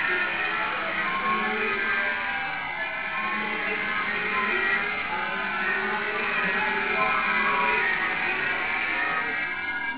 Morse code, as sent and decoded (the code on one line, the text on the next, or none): none